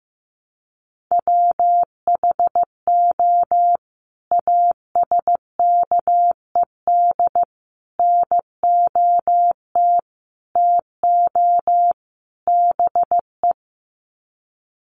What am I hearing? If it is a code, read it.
Morse code, heard as a keyed tone, about 15 wpm, 700 Hz. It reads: WHO ASKED NOT TO BE
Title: Sets of 5 Words Encore2 15wpm